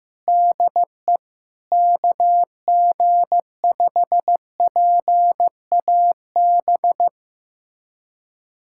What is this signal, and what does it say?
Morse code, heard as a keyed tone, about 15 wpm, 700 Hz. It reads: DE KG5PAB
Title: USA Callsign DE Callsign - 6 Characters - 15wpm